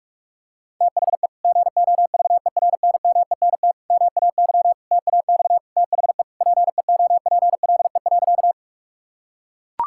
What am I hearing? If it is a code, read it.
Morse code, heard as a keyed tone, about 41 wpm, 700 Hz. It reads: THE GOVERNMENT MAY TAX THE PEOPLE.
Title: Sentences from Top 1000 Words - Review Part 5 - 45wpm